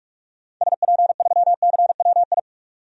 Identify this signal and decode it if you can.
Morse code, heard as a keyed tone, about 46 wpm, 690 Hz. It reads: SP3CWI
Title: DXCC-100 Call Signs - 6 Characters - 45wpm